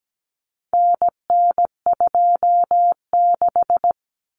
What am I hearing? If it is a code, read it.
Morse code, heard as a keyed tone, about 17 wpm, 700 Hz. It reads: NN26
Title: Common Cut Numbers - 4-Digit 17wpm